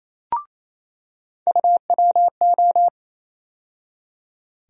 Morse code, heard as a keyed tone, about 28 wpm, 700 Hz. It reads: UWO
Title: USA Call Sign Postfixes - 3 Letters 28wpm